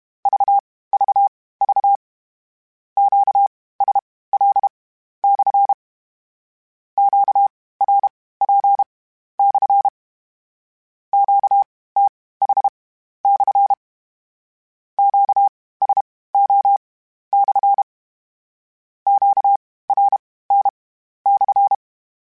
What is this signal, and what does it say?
Morse code, heard as a keyed tone, about 32 wpm, spaced as for 16 wpm, 800 Hz. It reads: VVV QSL/ QRP/ QTH/ QSO/ QRN/